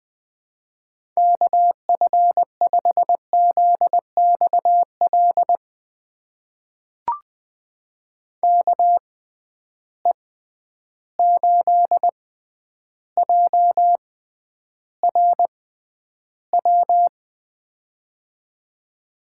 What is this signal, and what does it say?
Morse code, heard as a keyed tone, about 20 wpm, spaced as for 10 wpm, 700 Hz. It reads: KF5ZXL KE8JRW
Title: USA Call Signs - 6 Characters - 6x Character Spacing - 20wpm